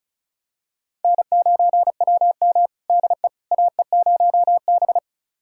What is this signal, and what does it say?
Morse code, heard as a keyed tone, about 36 wpm, 700 Hz. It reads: N9WM DE AE0B
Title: USA Callsign DE Callsign - 4 Characters - 35wpm